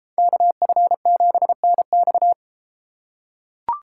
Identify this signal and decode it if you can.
Morse code, heard as a keyed tone, about 31 wpm, 700 Hz. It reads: KF7NX